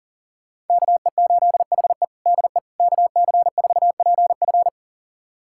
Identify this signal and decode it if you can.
Morse code, heard as a keyed tone, about 40 wpm, 700 Hz. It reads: KE8HE DE KC4PF
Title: USA Callsign DE Callsign - 5 Characters - 40wpm